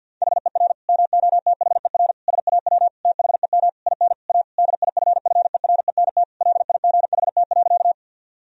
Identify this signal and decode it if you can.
Morse code, heard as a keyed tone, about 49 wpm, 700 Hz. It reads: HER MOTHER SAW THEM IN A DIFFERENT LIGHT.
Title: Sentences from Top 1000 Words - Review Part 3 - 50wpm